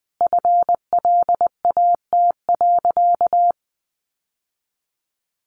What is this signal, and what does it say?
Morse code, heard as a keyed tone, about 19 wpm, 700 Hz. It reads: FLAT.
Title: Sentences from Top 1000 Words 20wpm